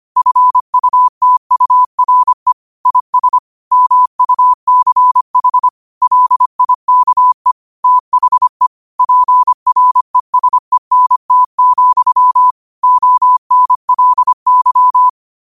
Morse code, heard as a keyed tone, about 25 wpm, 1000 Hz. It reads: FUTURE IS MUCH LIKE THE PRESENT, ONLY
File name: qotdfiles / 25wpm